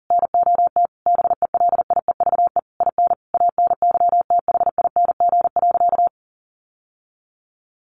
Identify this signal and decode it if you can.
Morse code, heard as a keyed tone, about 40 wpm, 700 Hz. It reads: NOT BELIEVE IN ANYTHING.
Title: Sentences from Top 1500 Words 40wpm